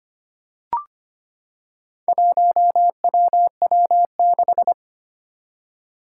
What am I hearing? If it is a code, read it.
Morse code, heard as a keyed tone, about 25 wpm, 700 Hz. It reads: 1WW6
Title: Four Challenging Letter-Number Combinations 25wpm